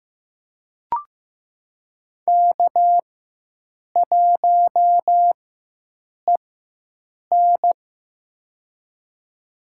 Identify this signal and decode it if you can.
Morse code, heard as a keyed tone, about 15 wpm, spaced as for 7 wpm, 700 Hz. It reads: K1EN
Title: USA Call Signs - 4 Characters - 4x Character Spacing - 15wpm